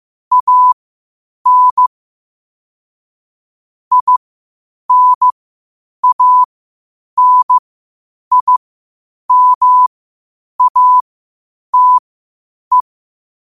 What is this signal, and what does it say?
Morse code, heard as a keyed tone, about 15 wpm, spaced as for 8 wpm, 1000 Hz. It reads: AN INANIMATE
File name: qotdfiles / 8wpm